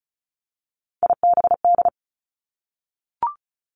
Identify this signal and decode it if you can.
Morse code, heard as a keyed tone, about 36 wpm, 700 Hz. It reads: IBD
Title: Three Challenging Letter-Number Combinations 35wpm